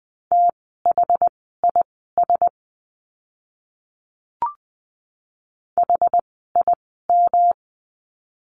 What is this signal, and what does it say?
Morse code, heard as a keyed tone, about 20 wpm, spaced as for 15 wpm, 700 Hz. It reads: THIS HIM
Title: 160 Lesson 26f - Letter g - Words 20wpm